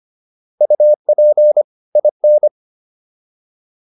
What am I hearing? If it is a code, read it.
Morse code, heard as a keyed tone, about 25 wpm, 600 Hz. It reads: UP IN